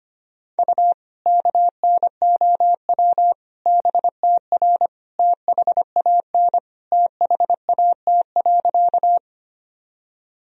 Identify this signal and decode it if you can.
Morse code, heard as a keyed tone, about 25 wpm, 700 Hz. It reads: U KNOW BTR THAN THAT.
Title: Sentences from Top 400 Words and Abbreviations 25wpm